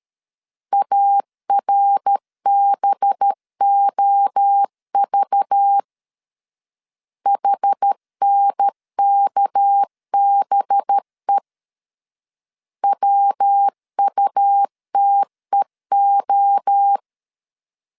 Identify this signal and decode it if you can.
Morse code, heard as a keyed tone, about 13 wpm, 800 Hz. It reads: ARBOV HNKBE WUTEO
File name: LEZIONI / 6 PKX0 / 6 PK 12wpm 2006